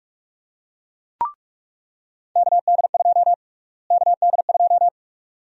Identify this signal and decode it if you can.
Morse code, heard as a keyed tone, about 46 wpm, 700 Hz. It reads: KD2 KD2
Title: USA Call Sign Prefixes - Repeated 2x - 45wpm